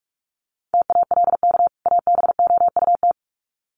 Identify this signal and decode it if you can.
Morse code, heard as a keyed tone, about 44 wpm, 700 Hz. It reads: TALK ABOUT
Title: Sets of 2 Words 45wpm